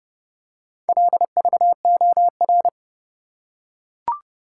Morse code, heard as a keyed tone, about 30 wpm, 700 Hz. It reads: LVOR